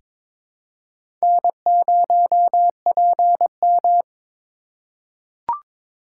Morse code, heard as a keyed tone, about 22 wpm, 700 Hz. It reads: N0PM
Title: USA Call Signs - 4 Characters - 4x Character Spacing - 22wpm